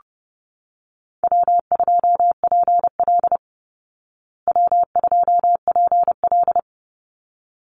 Morse code, heard as a keyed tone, about 30 wpm, 700 Hz. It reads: W2PL W2PL
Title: Call Signs Repeated 3x - 30wpm